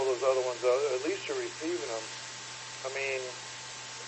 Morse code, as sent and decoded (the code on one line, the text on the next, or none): none